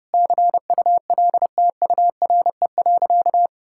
Morse code, heard as a keyed tone, about 32 wpm, 700 Hz. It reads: CULTURE.
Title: Sentences from Top 1000 Words - Review Part 6 - 30wpm